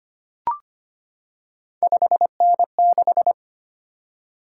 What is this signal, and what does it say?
Morse code, heard as a keyed tone, about 25 wpm, 700 Hz. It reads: 5N6
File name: Three Letter-Number Combinations - 25wpm